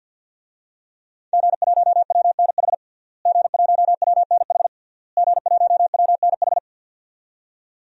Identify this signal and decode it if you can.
Morse code, heard as a keyed tone, about 52 wpm, 700 Hz. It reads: G1WNH G1WNH G1WNH
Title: DXCC-100 Call Signs - 5 Characters - Repeated 3x - 50wpm